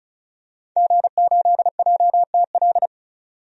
Morse code, heard as a keyed tone, about 36 wpm, 700 Hz. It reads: G8JTL
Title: DXCC-100 Call Signs - 5 Characters - 35wpm